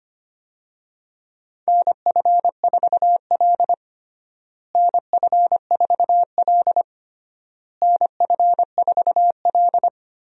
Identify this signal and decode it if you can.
Morse code, heard as a keyed tone, about 25 wpm, 700 Hz. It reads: NF4L NF4L NF4L
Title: USA Call Signs - 4 Characters - Repeated 3x - 25wpm